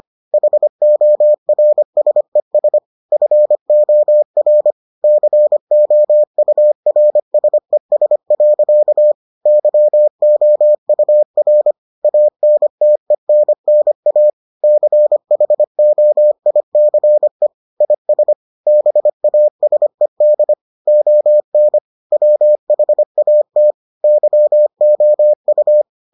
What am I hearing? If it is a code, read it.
Morse code, heard as a keyed tone, about 25 wpm, 600 Hz. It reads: HORSES FOR COURSES. YOUR ANTENNA CHOICE IS BASED ON WHAT YOU